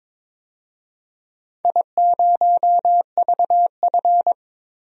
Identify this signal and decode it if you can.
Morse code, heard as a keyed tone, about 22 wpm, 700 Hz. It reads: I0VF
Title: DXCC-100 Call Signs - 4 Characters - Repeated 3x - 22wpm